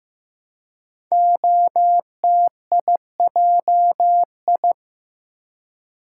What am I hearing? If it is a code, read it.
Morse code, heard as a keyed tone, about 15 wpm, 700 Hz. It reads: OTIJI
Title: Code Groups - 5 Letters - 15wpm